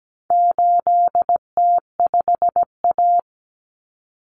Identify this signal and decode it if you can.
Morse code, heard as a keyed tone, about 17 wpm, 700 Hz. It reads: 8T5A
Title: Common Cut Numbers - 4-Digit 17wpm